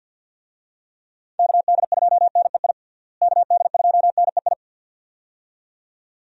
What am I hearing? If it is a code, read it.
Morse code, heard as a keyed tone, about 48 wpm, 700 Hz. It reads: KD2NEI KD2NEI
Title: USA Call Signs - 6 Characters - Repeated 2x - 50wpm